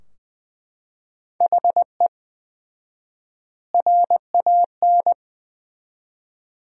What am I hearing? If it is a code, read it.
Morse code, heard as a keyed tone, about 20 wpm, 700 Hz. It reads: HE RAN
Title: 4x Word-Spacing - Sentences from Top 500 Words - 20wpm